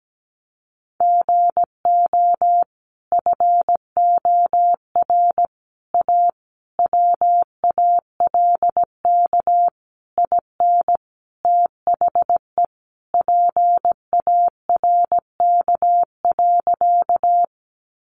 Morse code, heard as a keyed tone, about 17 wpm, 700 Hz. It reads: GO FOR A WALK IN THE PARK.
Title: Sentences from Top 1300 Words 17wpm